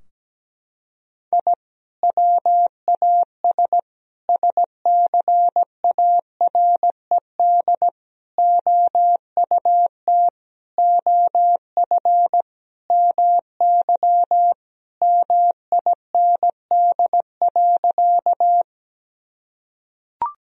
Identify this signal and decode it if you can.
Morse code, heard as a keyed tone, about 17 wpm, 700 Hz. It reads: I WAS SCARED OUT OF MY MIND.
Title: Sentences from Top 1000 Words - Review Part 4 - 17wpm